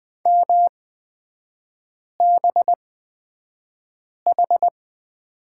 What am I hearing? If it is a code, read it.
Morse code, heard as a keyed tone, about 20 wpm, spaced as for 5 wpm, 700 Hz. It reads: MBH
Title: 255 Lesson 41e - Period - Three Characters 20wpm